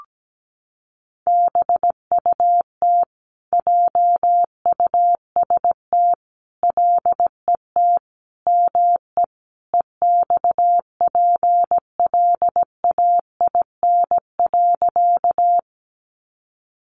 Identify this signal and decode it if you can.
Morse code, heard as a keyed tone, about 17 wpm, 700 Hz. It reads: BUT JUST LET ME EXPLAIN.
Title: Sentences from Top 1100 Words 17wpm